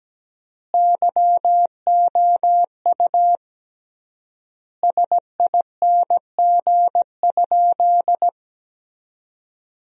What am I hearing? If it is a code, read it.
Morse code, heard as a keyed tone, about 17 wpm, 700 Hz. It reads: YOU SING?
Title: 3x Word-Spacing - Sentences from Top 400 Words - 17wpm